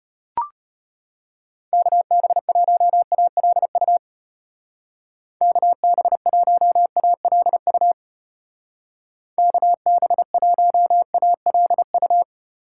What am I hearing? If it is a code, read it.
Morse code, heard as a keyed tone, about 32 wpm, 700 Hz. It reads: KB1ALU KB1ALU KB1ALU